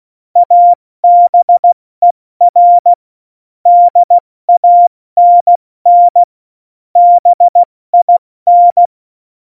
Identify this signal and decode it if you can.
Morse code, heard as a keyed tone, about 16 wpm, 700 Hz. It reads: ABER DANN BIN
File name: Morsekurs / Witze / Witz 02. Juli 2019, 70 F80